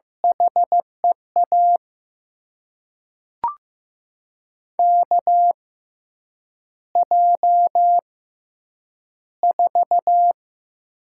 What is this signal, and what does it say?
Morse code, heard as a keyed tone, about 15 wpm, spaced as for 8 wpm, 700 Hz. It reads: HEA KJ4
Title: USA Call Signs - 6 Characters - 6x Character Spacing - 15wpm